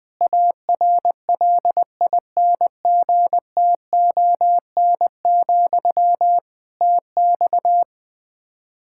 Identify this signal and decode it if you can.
Morse code, heard as a keyed tone, about 20 wpm, 700 Hz. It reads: ARLINGTON, TX
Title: Practice QSO - Exchange 1 - Beginning - 20wpm